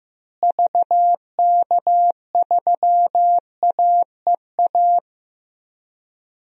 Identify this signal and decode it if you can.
Morse code, heard as a keyed tone, about 15 wpm, 700 Hz. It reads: VK3AEA
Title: DXCC-100 Call Signs - 6 Characters - Repeated 2x - 15wpm